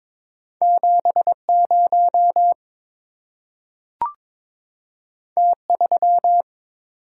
Morse code, heard as a keyed tone, about 22 wpm, 690 Hz. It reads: 70 T3
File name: practice / 22wpm / Two Letter-Number Combinations - 22wpm